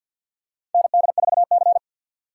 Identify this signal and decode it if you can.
Morse code, heard as a keyed tone, about 53 wpm, 700 Hz. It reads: ND4C
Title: USA Call Signs - 4 Characters - 6x Character Spacing - 50wpm